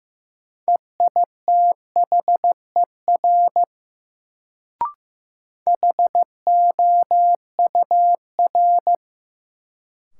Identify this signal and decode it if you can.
Morse code, heard as a keyed tone, about 15 wpm, 700 Hz. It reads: EITHER HOUR
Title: Sets of 5 Words - New Word Drill - 15wpm